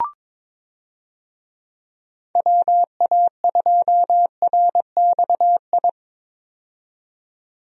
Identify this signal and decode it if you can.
Morse code, heard as a keyed tone, about 23 wpm, 700 Hz. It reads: WA2RXI
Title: USA Call Signs - 6 Characters - 22wpm